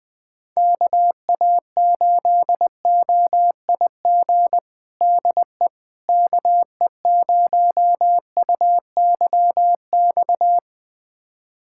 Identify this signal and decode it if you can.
Morse code, heard as a keyed tone, about 20 wpm, 700 Hz. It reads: KA8OIG DE KE0UYX
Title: USA Callsign DE Callsign - 6 Characters - 20wpm